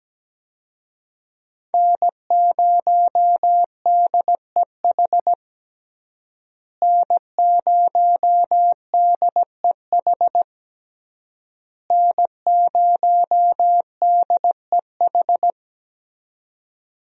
Morse code, heard as a keyed tone, about 17 wpm, 700 Hz. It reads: N0DEH N0DEH N0DEH
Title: USA Call Signs - 5 Characters - Repeated 3x - 17wpm